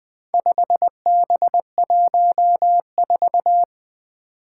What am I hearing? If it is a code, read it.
Morse code, heard as a keyed tone, about 20 wpm, 700 Hz. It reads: 5B14
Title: Four Challenging Letter-Number Combinations 20wpm